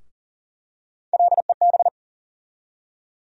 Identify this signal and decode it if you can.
Morse code, heard as a keyed tone, about 40 wpm, 700 Hz. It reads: LEB
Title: USA Call Sign Postfixes - 3 Letters 40wpm